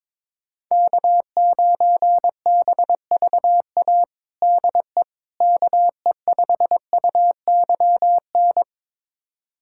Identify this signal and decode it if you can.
Morse code, heard as a keyed tone, about 22 wpm, 700 Hz. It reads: K9BVA DE KE5UYN
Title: USA Callsign DE Callsign 22wpm